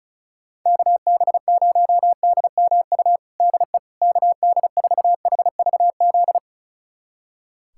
Morse code, heard as a keyed tone, about 36 wpm, 700 Hz. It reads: KB0DMU DE KD4HVZ
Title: USA Callsign DE Callsign 35wpm